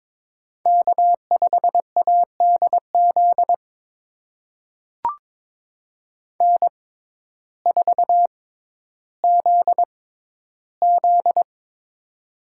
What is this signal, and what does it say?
Morse code, heard as a keyed tone, about 22 wpm, spaced as for 12 wpm, 700 Hz. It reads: K5ADZ N4ZZ
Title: USA Call Signs - 6x Character Spacing - 22wpm